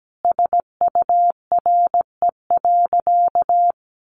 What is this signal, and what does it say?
Morse code, heard as a keyed tone, about 17 wpm, 700 Hz. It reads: SURE.
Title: Sentences from Top 500 Words 17wpm